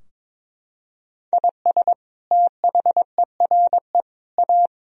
Morse code, heard as a keyed tone, about 22 wpm, 700 Hz. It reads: IS THERE A
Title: Sentences from Top 1000 Words - Review Part 4 - 22wpm